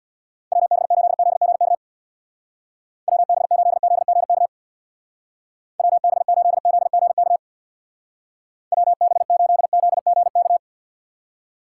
Morse code, heard as a keyed tone, about 66 wpm, 700 Hz. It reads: WB8ZGK WB8ZGK WB8ZGK WB8ZGK